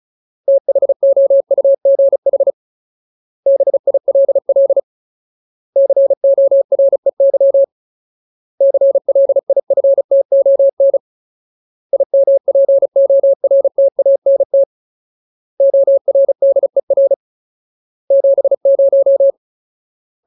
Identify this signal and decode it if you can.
Morse code, heard as a keyed tone, about 34 wpm, 550 Hz. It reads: THOUGH BILL COREY CLIFTON IMPORTANT ORDER 70
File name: Ultimate Word List / 35WPM Ultimate